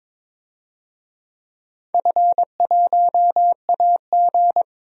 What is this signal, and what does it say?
Morse code, heard as a keyed tone, about 21 wpm, 700 Hz. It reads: F1AG
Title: DXCC-100 Call Signs - 4 Characters - 22wpm